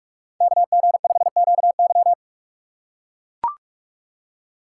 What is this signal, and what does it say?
Morse code, heard as a keyed tone, about 44 wpm, 690 Hz. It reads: KG5QY
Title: USA Call Signs - 5 Characters - 6x Character Spacing - 45wpm